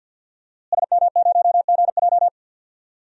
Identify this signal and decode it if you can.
Morse code, heard as a keyed tone, about 48 wpm, 700 Hz. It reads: SM0GJ
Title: DXCC-100 Call Signs - 5 Characters - Repeated 3x - 50wpm